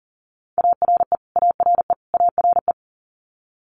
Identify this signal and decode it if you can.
Morse code, heard as a keyed tone, about 40 wpm, 700 Hz. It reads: ARE ARE ARE
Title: Top 200 Words Repeated 40wpm